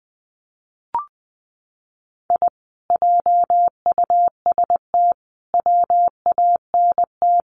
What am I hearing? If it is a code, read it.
Morse code, heard as a keyed tone, about 20 wpm, 700 Hz. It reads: I JUST WANT
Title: Sets of 5 Words Encore2 20wpm